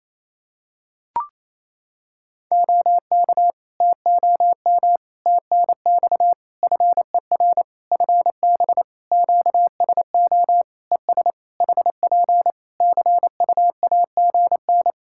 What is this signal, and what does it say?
Morse code, heard as a keyed tone, about 28 wpm, 700 Hz. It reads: OK TOM TNX FER FB QSO ES HP CUAGN
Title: Practice QSO - Exchange 4 - Ending - 28wpm